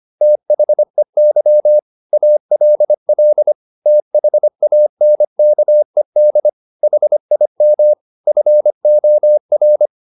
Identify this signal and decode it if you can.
Morse code, heard as a keyed tone, about 25 wpm, 600 Hz. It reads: THEY ALL THANKED HIM FOR